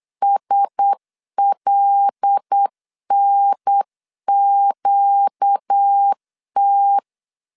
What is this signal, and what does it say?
Morse code, heard as a keyed tone, about 8 wpm, 800 Hz. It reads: SLNQT
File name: LEZIONI / 6 PKX0 / 6 PKX 8wpm 3041